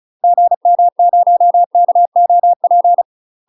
Morse code, heard as a keyed tone, about 34 wpm, 700 Hz. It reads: GM0KOP
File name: UK MASTER QSOs TRAINING / 35WPM / UK MASTER QSOs 0037